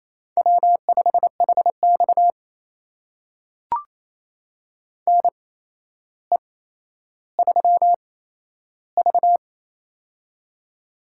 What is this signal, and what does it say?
Morse code, heard as a keyed tone, about 28 wpm, spaced as for 12 wpm, 700 Hz. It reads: W5HX NE3V
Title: USA Call Signs - 4 Characters - 8x Character Spacing - 28wpm